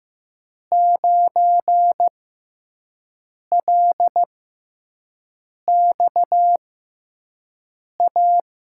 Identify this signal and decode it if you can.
Morse code, heard as a keyed tone, about 15 wpm, spaced as for 5 wpm, 700 Hz. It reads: 9LXA
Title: USA Call Signs - 5 Characters - 6x Character Spacing - 15wpm